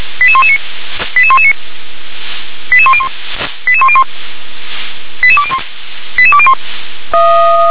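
Five-tone selective call (ZVEI), heard as quick runs of five tones, tones of about 70 ms. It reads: 88119 88119 88101 88101 88201 88201
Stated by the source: ZVEI five-tone selective call received on SDR.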